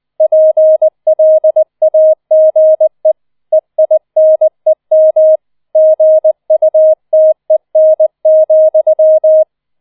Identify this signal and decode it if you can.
Morse code, heard as a keyed tone, about 20 wpm, 630 Hz. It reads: PLAGE EINEM GUTEN,